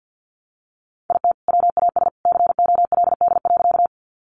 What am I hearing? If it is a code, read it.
Morse code, heard as a keyed tone, about 50 wpm, 700 Hz. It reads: IT WAS COLD.
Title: Sentences from Top 400 Words 50wpm